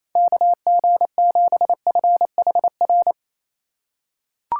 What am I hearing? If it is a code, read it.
Morse code, heard as a keyed tone, about 28 wpm, 700 Hz. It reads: KG7FHR
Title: USA Call Signs - 6 Characters - 3x Character Spacing - 28wpm